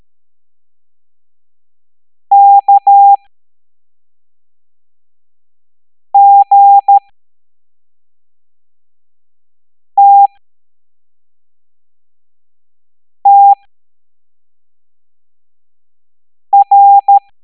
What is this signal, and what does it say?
Morse code, heard as a keyed tone, about 13 wpm, spaced as for 3 wpm, 800 Hz. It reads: KGTTR